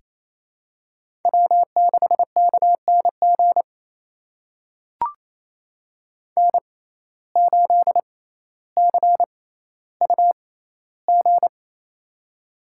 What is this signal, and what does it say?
Morse code, heard as a keyed tone, about 28 wpm, spaced as for 14 wpm, 700 Hz. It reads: W6KNG N8CUG
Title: USA Call Signs - 5 Characters - 6x Character Spacing - 28wpm